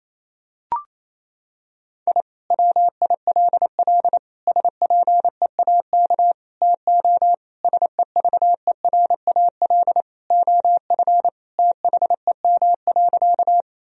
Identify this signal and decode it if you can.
Morse code, heard as a keyed tone, about 28 wpm, 700 Hz. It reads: I WILL SPEAK TO SEVERAL OF THEM.
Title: Sentences from Top 1000 Words - Review Part 2 - 28wpm